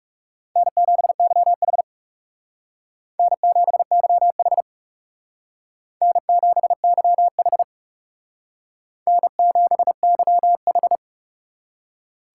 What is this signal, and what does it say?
Morse code, heard as a keyed tone, about 35 wpm, 700 Hz. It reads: N7YH N7YH N7YH N7YH
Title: Speed Racing - USA Call Signs 30wpm